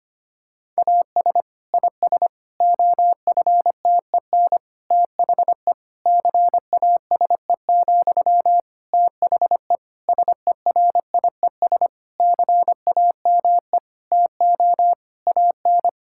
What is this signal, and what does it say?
Morse code, heard as a keyed tone, about 25 wpm, 700 Hz. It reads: AS IS OFTEN THE CASE, THE SERIES CAME TO AN
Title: Sentences from Top 1000 Words - Review Part 5 - 25wpm